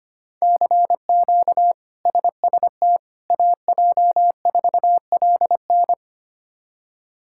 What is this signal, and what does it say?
Morse code, heard as a keyed tone, about 25 wpm, 700 Hz. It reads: CQ SST AJ4LN
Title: K1USN SST QSO Practice - 25wpm